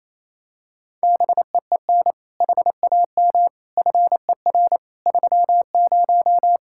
Morse code, heard as a keyed tone, about 27 wpm, 700 Hz. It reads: BEEN HAM FER 30
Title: Practice QSO - Exchange 3 - More Info - 28wpm